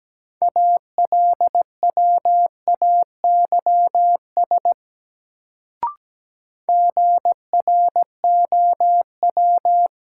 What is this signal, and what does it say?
Morse code, heard as a keyed tone, about 17 wpm, 700 Hz. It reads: ALWAYS GROW
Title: Sentences from Top 300 Words - New Word Drill - 17wpm